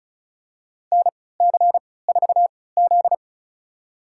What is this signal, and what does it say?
Morse code, heard as a keyed tone, about 36 wpm, spaced as for 20 wpm, 690 Hz. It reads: NC4Z
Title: USA Call Signs - 4 Characters - 3x Character Spacing - 35wpm